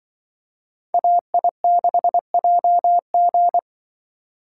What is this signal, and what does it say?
Morse code, heard as a keyed tone, about 25 wpm, 700 Hz. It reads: AI6JG
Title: Speed Racing - USA 2x2 Call Signs 20wpm